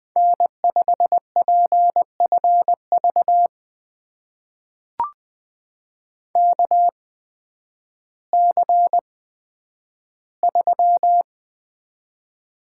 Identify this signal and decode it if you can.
Morse code, heard as a keyed tone, about 20 wpm, spaced as for 11 wpm, 700 Hz. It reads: N5PFV KC3
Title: USA Call Signs - 5 Characters - 8x Character Spacing - 20wpm